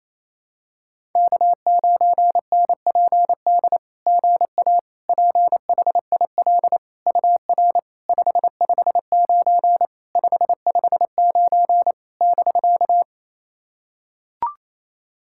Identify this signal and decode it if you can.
Morse code, heard as a keyed tone, about 28 wpm, 690 Hz. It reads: K9NPD GA PHIL UR 559 559 <BK>
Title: SOTA QSO Practice 28wpm